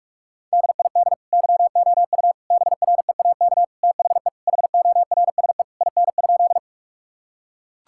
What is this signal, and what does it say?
Morse code, heard as a keyed tone, about 41 wpm, 700 Hz. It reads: DID YOU BREAK THE HORSE IN?
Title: Sentences from Top 1000 Words - Review Part 6 - 45wpm